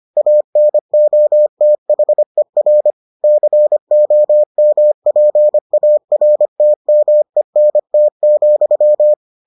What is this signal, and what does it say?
Morse code, heard as a keyed tone, about 25 wpm, 600 Hz. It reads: ANOTHER COMPARTMENT,